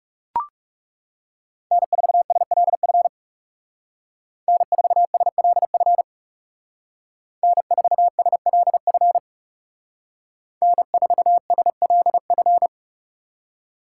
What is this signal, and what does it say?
Morse code, heard as a keyed tone, about 37 wpm, 700 Hz. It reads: N4SLF N4SLF N4SLF N4SLF
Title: Speed Racing - USA 1x3 Call Signs 30wpm